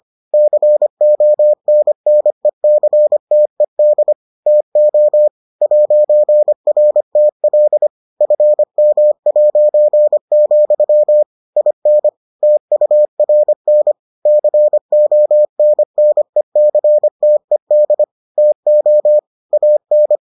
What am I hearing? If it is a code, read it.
Morse code, heard as a keyed tone, about 25 wpm, 600 Hz. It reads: CONNECTED TO 'RTL FM', IN TURN CONNECTED TO AN